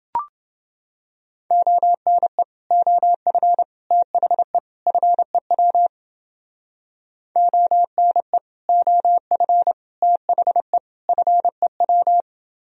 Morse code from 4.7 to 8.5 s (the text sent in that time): FEW ONE